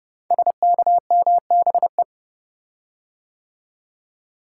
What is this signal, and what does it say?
Morse code, heard as a keyed tone, about 30 wpm, 700 Hz. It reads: SKMBE
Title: Code Groups - 5 Letters - 30wpm